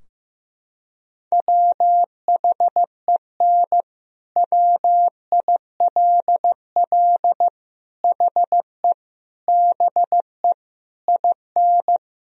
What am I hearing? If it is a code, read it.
Morse code, heard as a keyed tone, about 16 wpm, 700 Hz. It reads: WHEN WILL HE BE IN
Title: Sentences from Top 1000 Words - Review Part 4 - 15wpm